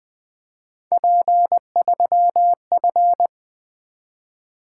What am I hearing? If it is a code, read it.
Morse code, heard as a keyed tone, about 20 wpm, 700 Hz. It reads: P3F
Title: Three Letter-Number Combinations - 20wpm